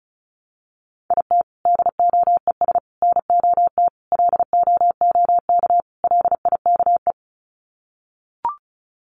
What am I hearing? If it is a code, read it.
Morse code, heard as a keyed tone, about 34 wpm, 700 Hz. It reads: IT DOES NOT LOOK LIKE
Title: Sets of 5 Words Encore2 35wpm